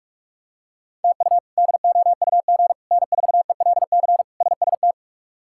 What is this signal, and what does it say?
Morse code, heard as a keyed tone, about 41 wpm, 700 Hz. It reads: TU DOUG N4ELC SST
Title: K1USN SST QSO Practice - 45wpm